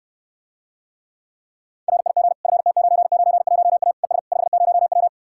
Morse code, heard as a keyed tone, about 69 wpm, 700 Hz. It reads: RIG FT991A ES 50W